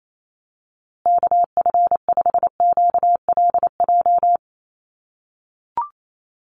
Morse code, heard as a keyed tone, about 28 wpm, 700 Hz. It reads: KF5QLJ
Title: USA Call Signs Repeated 3x 28wpm